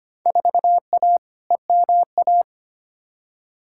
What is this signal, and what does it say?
Morse code, heard as a keyed tone, about 25 wpm, 700 Hz. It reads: 4A EMA
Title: Field Day Exchanges - 25wpm